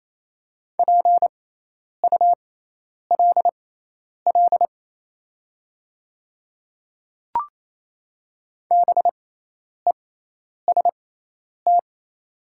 Morse code, heard as a keyed tone, about 28 wpm, spaced as for 10 wpm, 700 Hz. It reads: PULL BEST
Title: 6x Character Spacing - Top 500 Words - 28wpm